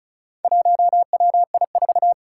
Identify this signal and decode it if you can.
Morse code, heard as a keyed tone, about 36 wpm, 700 Hz. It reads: 1WI4
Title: Four Challenging Letter-Number Combinations 35wpm